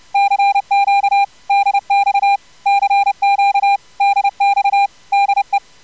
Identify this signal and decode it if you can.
Morse code, heard as a keyed tone, about 30 wpm, 800 Hz. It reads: CQ DX CQ DX DE
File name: CW jitter with corr-100-10-50